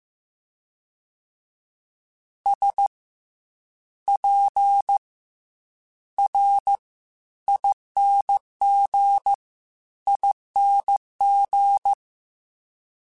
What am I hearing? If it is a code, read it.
Morse code, heard as a keyed tone, about 15 wpm, 800 Hz. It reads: S P R ING ING
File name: Special INGs / Special INGs-mp3s / SlowToFaster ing @13